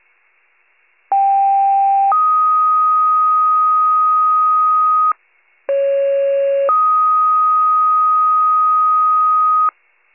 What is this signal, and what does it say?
Two-tone page, heard as a one-second tone, then a three-second tone, about 790 then 1250 Hz, about 550 then 1250 Hz. It is heard from 1.1 to 9.7 s.